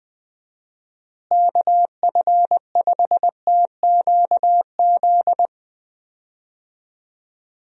Morse code, heard as a keyed tone, about 20 wpm, 700 Hz. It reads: KF5TQZ
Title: USA Call Signs - 6 Characters - Repeated 2x - 20wpm